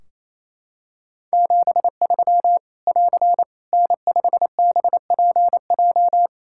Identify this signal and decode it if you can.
Morse code, heard as a keyed tone, about 27 wpm, 700 Hz. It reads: 73 <AR> N5BPJ
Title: Practice QSO - Exchange 4 - Ending - 28wpm